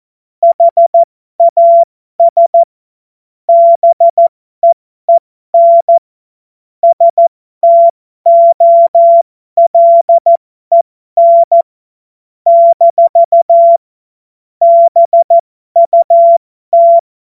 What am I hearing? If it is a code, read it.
Morse code, heard as a keyed tone, about 14 wpm, spaced as for 12 wpm, 680 Hz. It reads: HAS BEEN STOLEN - BUT
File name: Morsekurs / Witze